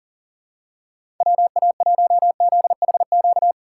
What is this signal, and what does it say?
Morse code, heard as a keyed tone, about 41 wpm, 700 Hz. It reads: WA1ZHQ
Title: USA Call Signs - 6 Characters - 6x Character Spacing - 40wpm